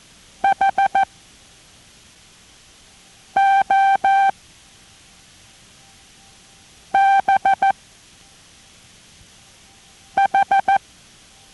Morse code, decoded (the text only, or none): HOBH